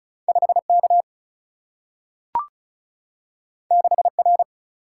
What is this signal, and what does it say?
Morse code, heard as a keyed tone, about 36 wpm, 700 Hz. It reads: 5K 6R
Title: Single-Letter Single-Number 35wpm